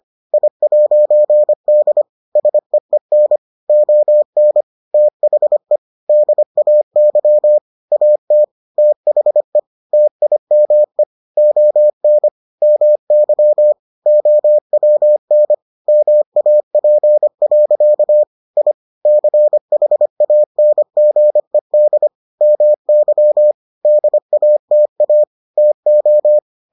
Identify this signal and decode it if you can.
Morse code, heard as a keyed tone, about 25 wpm, 600 Hz. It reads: I'D SEEN ON THE DAY AT THE TIME ON MY OWN MAP. I CHANGED MY DATA TO